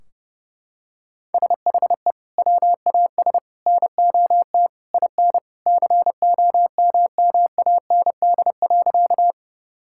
Morse code, heard as a keyed tone, about 30 wpm, 700 Hz. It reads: SHE WAS NOT IN COMMAND.
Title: Sentences from Top 1600 Words 30wpm